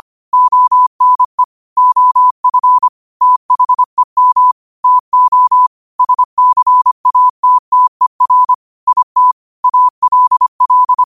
Morse code, heard as a keyed tone, about 25 wpm, 1000 Hz. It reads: ONE OF THEM TO SCATTER IT ALL
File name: qotdfiles / 25wpm